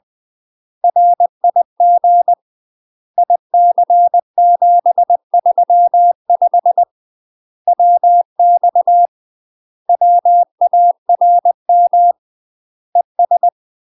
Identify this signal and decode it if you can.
Morse code, heard as a keyed tone, about 20 wpm, 700 Hz. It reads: RIG IC735 WX WARM ES
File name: UK MASTER QSOs TRAINING / 20WPM / UK MASTER QSOs 0062